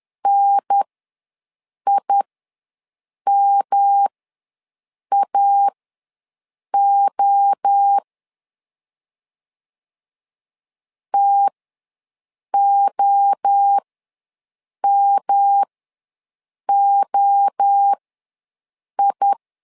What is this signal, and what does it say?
Morse code, heard as a keyed tone, about 10 wpm, spaced as for 6 wpm, 800 Hz. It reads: NIMAO TOMOI
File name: LEZIONI / 3 MWRO / 3 MWRO ws 10wpm 2310